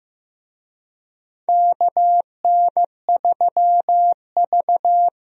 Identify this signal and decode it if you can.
Morse code, heard as a keyed tone, about 15 wpm, 700 Hz. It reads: KN3V
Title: USA Call Signs - 4 Characters - Repeated 3x - 15wpm